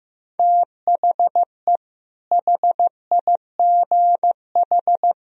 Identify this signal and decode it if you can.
Morse code, heard as a keyed tone, about 16 wpm, 700 Hz. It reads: THE HIGH
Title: Sentences from Top 1000 Words - Review Part 1 - 15wpm